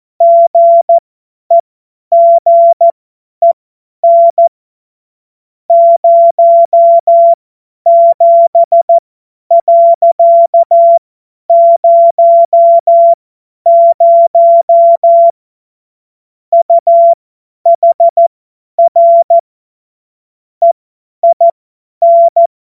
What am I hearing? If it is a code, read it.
Morse code, heard as a keyed tone, about 14 wpm, spaced as for 10 wpm, 680 Hz. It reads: GEGEN 07.00 UHR EIN
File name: Morsekurs / Witze / Witz 09. November 2021, 50 F70